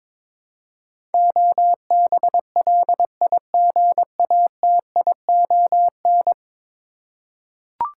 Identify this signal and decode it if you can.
Morse code, heard as a keyed tone, about 22 wpm, 700 Hz. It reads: OBLIGATION
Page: morse-code-ninja-practice-22wpm